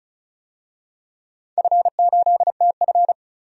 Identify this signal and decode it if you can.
Morse code, heard as a keyed tone, about 36 wpm, 700 Hz. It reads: F8TF
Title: DXCC-100 Call Signs - 4 Characters - 35wpm